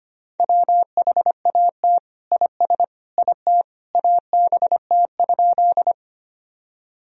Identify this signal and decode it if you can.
Morse code, heard as a keyed tone, about 25 wpm, 700 Hz. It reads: WHAT IS IT ABT?
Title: Sentences from Top 100 Words and Abbreviations 25wpm